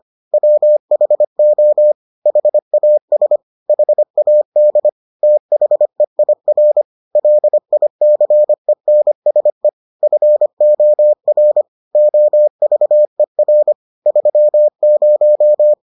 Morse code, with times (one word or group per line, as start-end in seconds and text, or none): 0.3-1.9 WHO
2.3-3.4 HAS
3.7-4.9 HAD
5.2-6.8 THEIR
7.2-9.7 LICENSE
10.0-11.6 FOR
11.9-13.7 OVER
14.1-15.7 30